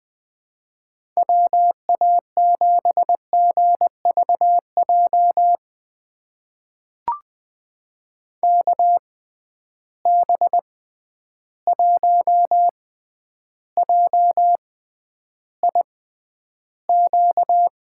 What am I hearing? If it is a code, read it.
Morse code, heard as a keyed tone, about 20 wpm, spaced as for 10 wpm, 700 Hz. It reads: WA7GVJ KB1JIQ
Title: USA Call Signs - 6 Characters - 6x Character Spacing - 20wpm